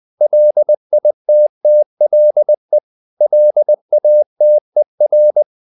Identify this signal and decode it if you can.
Morse code, heard as a keyed tone, about 20 wpm, 600 Hz. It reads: LITTLE LATER
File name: the old man and the sea part 2 20wpm